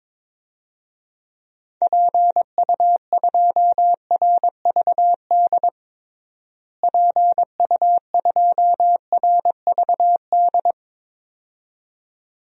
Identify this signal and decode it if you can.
Morse code, heard as a keyed tone, about 21 wpm, 700 Hz. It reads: PU2RVD PU2RVD
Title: DXCC-100 Call Signs - 6 Characters - Repeated 2x - 22wpm